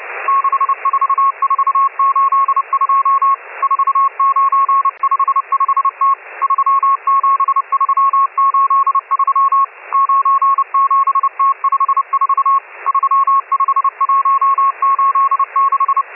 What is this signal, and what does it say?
Morse code, heard as a keyed tone, about 29 wpm, 1090 Hz, with radio noise. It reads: 64482 4955T 37383 97T54 35186
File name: M12- -1010utc